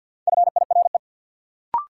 Morse code, heard as a keyed tone, about 51 wpm, 700 Hz. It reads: FIRE
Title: Sentences from Top 400 Words - New Word Drill - 50wpm